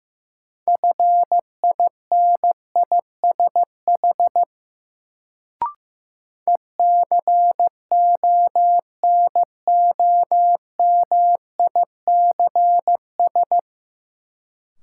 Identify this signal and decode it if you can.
Morse code, heard as a keyed tone, about 15 wpm, 700 Hz. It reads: FINISH ECONOMICS
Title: Sentences from Top 1000 Words - Review Part 3 - New Word Drill - 15wpm